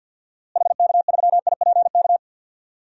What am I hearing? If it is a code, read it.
Morse code, heard as a keyed tone, about 52 wpm, 700 Hz. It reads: HK3IPK
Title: DXCC-100 Call Signs - 6 Characters - Repeated 3x - 50wpm